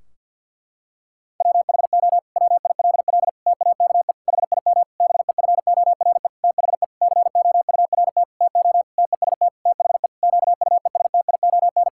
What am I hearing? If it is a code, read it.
Morse code, heard as a keyed tone, about 50 wpm, 700 Hz. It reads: WHO WILL TAKE HIM BEFORE THE COURT TO TEST THE QUESTION
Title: Sentences from Top 1600 Words 50wpm